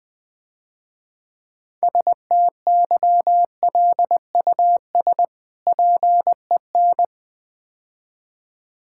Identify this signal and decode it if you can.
Morse code, heard as a keyed tone, about 22 wpm, 690 Hz. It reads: STYLUS PEN
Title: Difficult Words - Speed-Racing with Hints - 20wpm